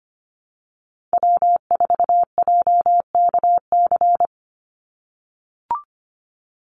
Morse code, heard as a keyed tone, about 25 wpm, 700 Hz. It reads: W4JKC
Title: NA SOTA Chaser Callsigns 25wpm